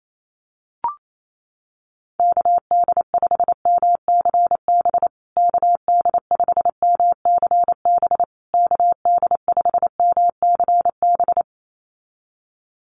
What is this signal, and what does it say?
Morse code, heard as a keyed tone, about 28 wpm, 690 Hz. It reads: KD5MCB KD5MCB KD5MCB
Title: USA Call Signs Repeated 3x 28wpm